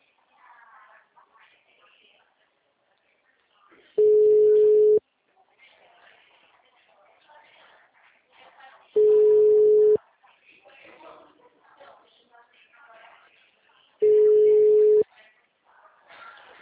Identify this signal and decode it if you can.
Ringback tone (Europe, 425 Hz), with speech.